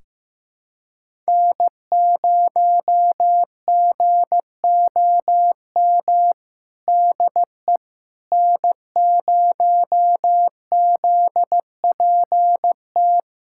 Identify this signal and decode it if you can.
Morse code, heard as a keyed tone, about 15 wpm, 700 Hz. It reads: N0GOM DE N0ZPT
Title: USA Callsign DE Callsign - 5 Characters - 15wpm